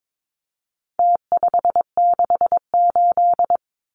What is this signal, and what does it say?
Morse code, heard as a keyed tone, about 22 wpm, 700 Hz. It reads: T568
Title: Numbers - Contest Style - 4-Digit 22wpm